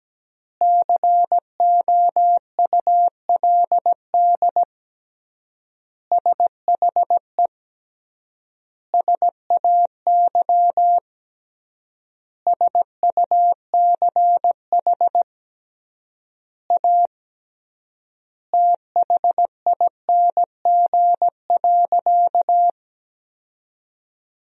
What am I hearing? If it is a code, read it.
Morse code, heard as a keyed tone, about 17 wpm, 700 Hz. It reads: COULD SHE SAY SUCH A THING.
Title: 3x Word-Spacing - Sentences from Top 200 Words - 17wpm